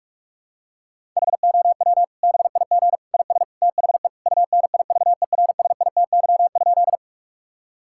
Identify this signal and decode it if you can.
Morse code, heard as a keyed tone, about 48 wpm, 700 Hz. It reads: HOW BIG IS THE UNIVERSITY?
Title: Sentences from Top 1000 Words - Review Part 5 - 45wpm